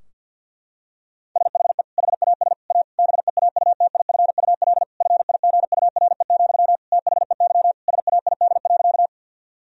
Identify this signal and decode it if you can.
Morse code, heard as a keyed tone, about 50 wpm, 700 Hz. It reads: SHE HAS A BEAUTIFUL FIGURE, THEY SAID.
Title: Sentences from Top 1000 Words - Review Part 2 - 50wpm